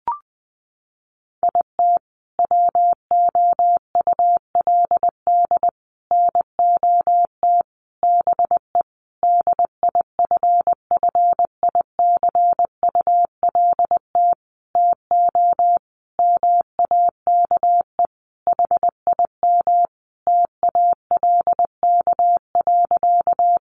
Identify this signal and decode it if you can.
Morse code, heard as a keyed tone, about 20 wpm, 700 Hz. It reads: IT WOULD NOT BE DIFFICULT TO MAKE HIM TALK.
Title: Sentences from Top 600 Words 20wpm